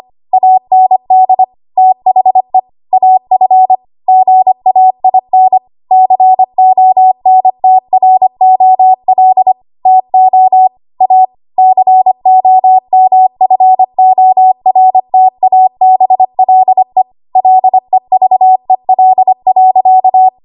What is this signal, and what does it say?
Morse code, heard as a keyed tone, about 25 wpm, 750 Hz. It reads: AND THE AF GAIN CONTROL TO A COMFORTABLE LEVEL.
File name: Archive / 25 WPM / 25WPM